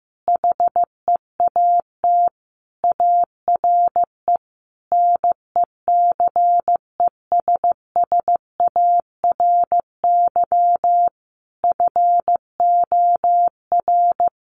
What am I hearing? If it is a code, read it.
Morse code, heard as a keyed tone, about 15 wpm, 700 Hz. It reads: HEAT ARE NECESSARY FOR
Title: Sentences from Top 1300 Words 15wpm